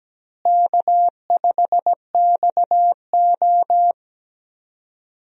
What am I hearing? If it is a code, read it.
Morse code, heard as a keyed tone, about 17 wpm, 700 Hz. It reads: K5XO
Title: USA Call Signs - 4 Characters - Repeated 3x - 17wpm